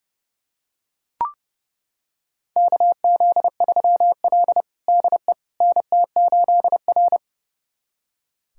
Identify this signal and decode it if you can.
Morse code, heard as a keyed tone, about 29 wpm, 700 Hz. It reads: KZ3L DE NT8R
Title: USA Callsign DE Callsign - 4 Characters - 30wpm